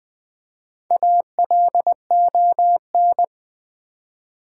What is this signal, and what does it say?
Morse code, heard as a keyed tone, about 20 wpm, 700 Hz. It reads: ALON
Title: Build A Word - 1 Letter At A Time - 5-Letter Words - 20wpm